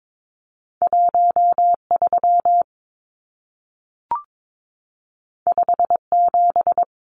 Morse code, heard as a keyed tone, about 22 wpm, 700 Hz. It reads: 13 57
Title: Numbers - 2-Digit 22wpm